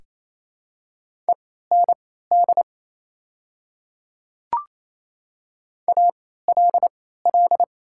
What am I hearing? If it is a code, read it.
Morse code, heard as a keyed tone, about 27 wpm, spaced as for 16 wpm, 700 Hz. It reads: END ALL
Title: 3x Character Spacing - Top 200 Words - 28wpm